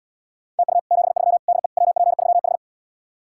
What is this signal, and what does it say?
Morse code, heard as a keyed tone, about 79 wpm, 700 Hz. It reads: TU 73 DE WP3H